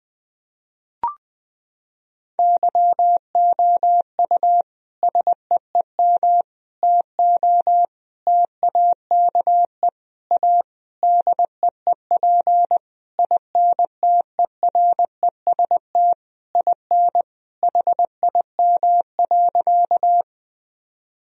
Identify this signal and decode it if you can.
Morse code, heard as a keyed tone, about 20 wpm, 700 Hz. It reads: YOU SEEM TO TAKE A DEEP INTEREST IN HIM.
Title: Sentences from Top 1000 Words - Review Part 3 - 20wpm